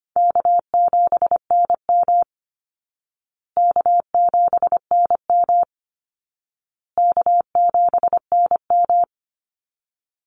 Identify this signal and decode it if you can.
Morse code, heard as a keyed tone, about 25 wpm, 700 Hz. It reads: K7NM K7NM K7NM
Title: Easy Call Signs Repeated 3x - 25wpm